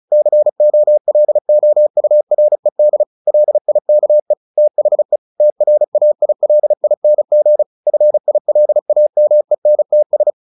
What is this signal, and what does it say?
Morse code, heard as a keyed tone, about 34 wpm, 600 Hz. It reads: COLOURED LIKE THE TRAILING FILAMENTS
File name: the old man and the sea / the old man and the sea 1 35wpm